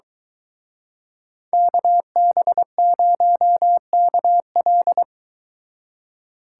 Morse code, heard as a keyed tone, about 23 wpm, 700 Hz. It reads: KB0KL